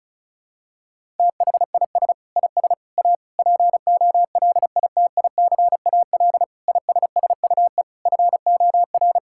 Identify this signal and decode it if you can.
Morse code, heard as a keyed tone, about 34 wpm, 700 Hz. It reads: THIS IS A POLITICAL ISSUE FOR
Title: Sentences from Top 1000 Words - Review Part 6 - 35wpm